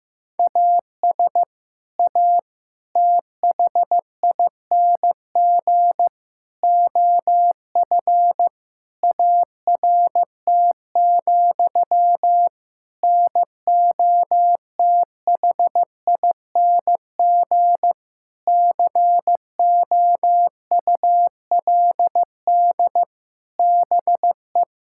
Morse code, heard as a keyed tone, about 16 wpm, 700 Hz. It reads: AS A THING OF ART, NOTHING COULD BE
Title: Sentences from Top 1000 Words - Review Part 6 - 15wpm